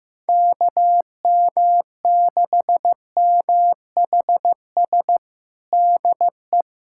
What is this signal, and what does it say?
Morse code, heard as a keyed tone, about 15 wpm, 700 Hz. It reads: KM6MHS DE
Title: USA Callsign DE Callsign - 6 Characters - 15wpm